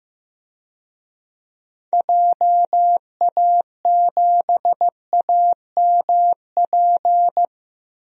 Morse code, heard as a keyed tone, about 15 wpm, 700 Hz. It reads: JA7AMP
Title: DXCC-100 Call Signs - 6 Characters - Repeated 3x - 15wpm